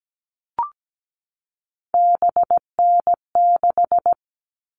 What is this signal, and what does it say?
Morse code, heard as a keyed tone, about 17 wpm, 700 Hz. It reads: BN6
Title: All Cut Numbers - 3-Digit 17wpm